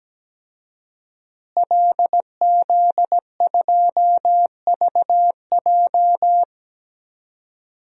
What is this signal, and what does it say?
Morse code, heard as a keyed tone, about 17 wpm, 700 Hz. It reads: LZ2VJ
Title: DXCC-100 Call Signs - 5 Characters - Repeated 2x - 17wpm